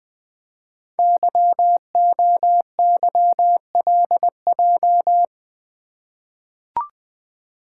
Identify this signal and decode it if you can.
Morse code, heard as a keyed tone, about 20 wpm, 700 Hz. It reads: YOYLJ
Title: Code Groups - 5 Letter:Numbers - 20wpm